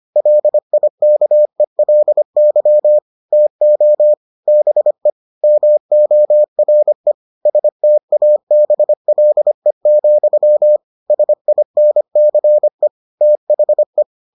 Morse code, heard as a keyed tone, about 25 wpm, 600 Hz. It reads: LIKELY TO BE MORE STABLE, SINCE THE